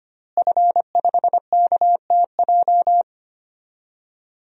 Morse code, heard as a keyed tone, about 25 wpm, 700 Hz. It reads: F5KTJ
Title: DXCC-100 Call Signs - 5 Characters - Repeated 3x - 25wpm